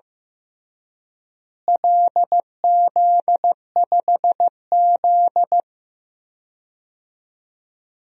Morse code, heard as a keyed tone, about 16 wpm, 700 Hz. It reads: LZ5Z
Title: DXCC-100 Call Signs - 4 Characters - Repeated 3x - 15wpm